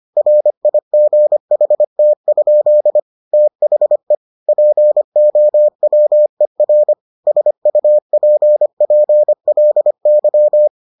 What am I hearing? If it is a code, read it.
Morse code, heard as a keyed tone, about 25 wpm, 600 Hz. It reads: RIGHT? THE POWER SUPPLY